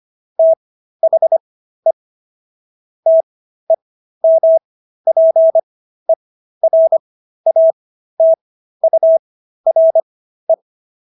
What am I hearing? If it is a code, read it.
Morse code, heard as a keyed tone, about 25 wpm, spaced as for 13 wpm, 650 Hz. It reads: THE TEMPERATURE